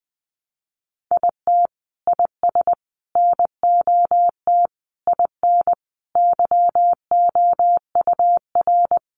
Morse code, heard as a keyed tone, about 20 wpm, 700 Hz. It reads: IT IS NOT IN YOUR
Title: Sentences from Top 1400 Words 20wpm